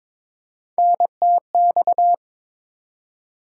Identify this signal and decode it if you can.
Morse code, heard as a keyed tone, about 22 wpm, 700 Hz. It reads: NTX
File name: ARRL Sections v2 - 22wpm